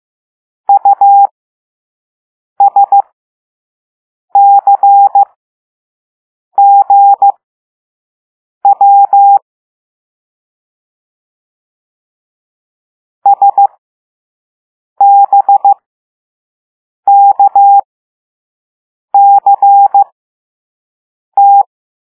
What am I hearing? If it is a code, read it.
Morse code, heard as a keyed tone, about 15 wpm, spaced as for 5 wpm, 800 Hz. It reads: USCGW SBKCT
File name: Morse / Level6-All-A